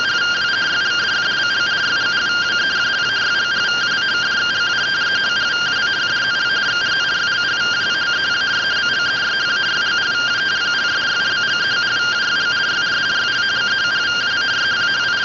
Two-tone teleprinter signal, about 170 Hz shift.